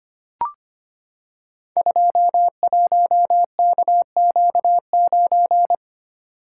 Morse code, heard as a keyed tone, about 25 wpm, 700 Hz. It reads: 21KQ9
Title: Code Groups - 5 Letter:Numbers - 25wpm